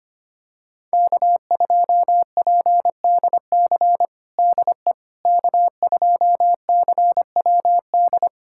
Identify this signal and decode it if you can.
Morse code, heard as a keyed tone, about 25 wpm, 700 Hz. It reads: K2PDC DE K2CWD
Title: USA Callsign DE Callsign - 5 Characters - 25wpm